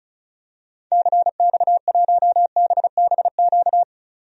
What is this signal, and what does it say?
Morse code, heard as a keyed tone, about 34 wpm, 700 Hz. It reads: CX1BBQ
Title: DXCC-100 Call Signs - 6 Characters - 35wpm